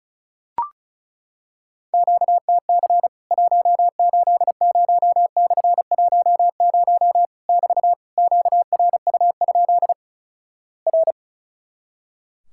Morse code, heard as a keyed tone, about 36 wpm, 700 Hz. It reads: QTC 180/10 <BT> QRU? R
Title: Worked All Europe DX Contest - QTC Exchange - 35wpm